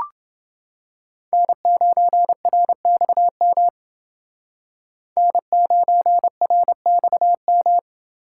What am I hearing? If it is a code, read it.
Morse code, heard as a keyed tone, about 29 wpm, 700 Hz. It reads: N9RXM N9RXM